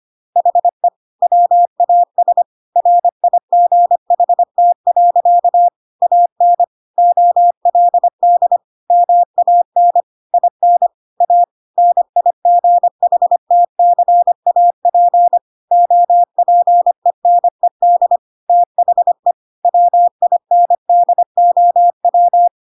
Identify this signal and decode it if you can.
Morse code, heard as a keyed tone, about 25 wpm, 700 Hz. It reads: HE WAS RIGHT. AN OLD MAN IN A NIGHTCAP OPENED THE WINDOW